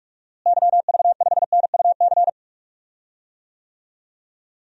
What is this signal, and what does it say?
Morse code, heard as a keyed tone, about 46 wpm, 700 Hz. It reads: YV5NUC
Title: DXCC-100 Call Signs - 6 Characters - Repeated 2x - 45wpm